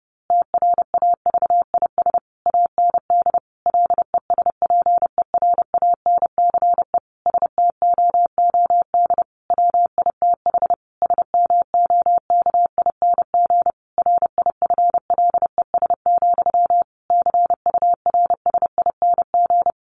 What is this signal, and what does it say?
Morse code, heard as a keyed tone, about 30 wpm, 700 Hz. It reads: TRAVIS AND LESPERANCE STOOD WITH SMOKING RIFLES, CURSING
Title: A Sound Of Thunder 30wpm